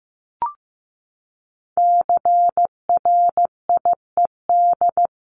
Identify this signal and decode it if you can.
Morse code, heard as a keyed tone, about 15 wpm, 700 Hz. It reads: CRIED
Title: A Sound Of Thunder 15wpm